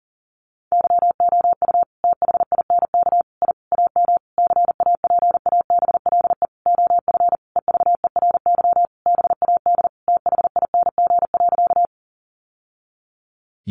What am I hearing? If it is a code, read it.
Morse code, heard as a keyed tone, about 40 wpm, 700 Hz. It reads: YOU THINK I AM CAPABLE OF EVERY BAD THING.
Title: Sentences from Top 1200 Words 40wpm